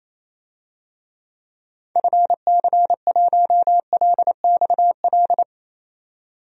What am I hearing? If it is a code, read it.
Morse code, heard as a keyed tone, about 28 wpm, 700 Hz. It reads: FC1LXL
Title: DXCC-100 Call Signs - 6 Characters - 28wpm